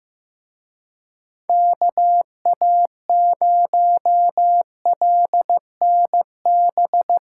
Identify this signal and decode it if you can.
Morse code, heard as a keyed tone, about 17 wpm, 700 Hz. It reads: KA0LNB
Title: Speed Racing - USA Call Signs 15wpm